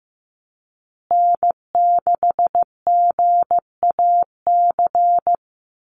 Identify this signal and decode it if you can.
Morse code, heard as a keyed tone, about 15 wpm, 700 Hz. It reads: N6GAC
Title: USA Call Signs Repeated 3x 15wpm